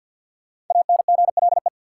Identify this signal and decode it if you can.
Morse code, heard as a keyed tone, about 53 wpm, 700 Hz. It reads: ANGLE
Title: Sentences from Top 1500 Words - New Word Drill Alt - 50wpm